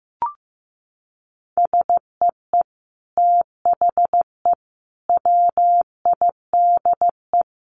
Morse code, heard as a keyed tone, about 15 wpm, 700 Hz. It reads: SEE THE WIDE
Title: Sentences from Top 900 Words 15wpm